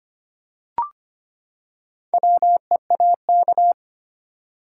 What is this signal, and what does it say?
Morse code, heard as a keyed tone, about 25 wpm, 700 Hz. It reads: WEAK
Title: Four Letter Words - 25wpm